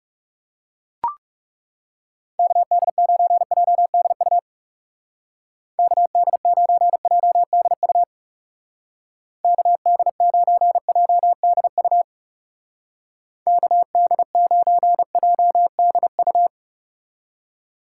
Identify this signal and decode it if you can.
Morse code, heard as a keyed tone, about 37 wpm, 700 Hz. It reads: KD9JDU KD9JDU KD9JDU KD9JDU
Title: Speed Racing - USA 2x3 Call Signs 30wpm